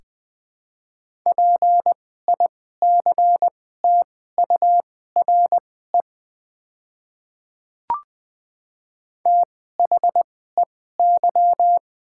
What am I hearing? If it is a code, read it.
Morse code, heard as a keyed tone, about 20 wpm, spaced as for 15 wpm, 690 Hz. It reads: PICTURE THEY
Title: 2x Character Spacing - Top 200 Words - 20wpm